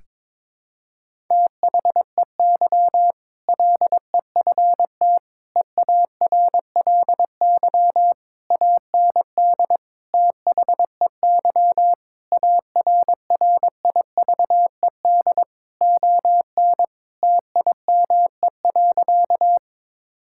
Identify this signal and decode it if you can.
Morse code, heard as a keyed tone, about 22 wpm, 700 Hz. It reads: THEY LEFT EARLY AND THEY ARRIVED ON TIME.
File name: practice / 22wpm / Sentences with 2-Word Phrases - 22wpm